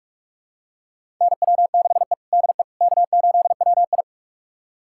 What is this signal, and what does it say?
Morse code, heard as a keyed tone, about 46 wpm, 700 Hz. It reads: NW6E DE K8WI
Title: USA Callsign DE Callsign - 4 Characters - 45wpm